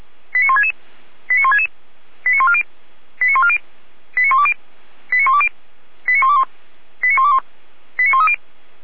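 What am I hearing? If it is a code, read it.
Five-tone selective call (ZVEI), heard as quick runs of five tones, tones of about 80 ms. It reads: 89166 89166 89140 89140 89120 89120 89121 89121 89130
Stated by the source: ZVEI five-tone selective call received by SDR.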